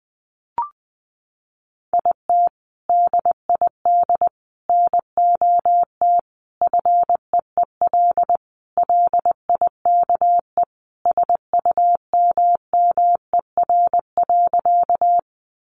Morse code, heard as a keyed tone, about 20 wpm, 700 Hz. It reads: IT DID NOT FEEL LIKE SUMMER.
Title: Sentences from Top 1100 Words 20wpm